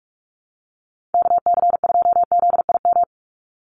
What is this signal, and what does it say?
Morse code, heard as a keyed tone, about 44 wpm, 700 Hz. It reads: KC2ZIM
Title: USA Call Signs 45wpm